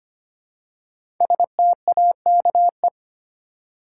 Morse code, heard as a keyed tone, about 25 wpm, 700 Hz. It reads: STAKE
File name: 5 Letter Words - First 3 Letters Given - 25wpm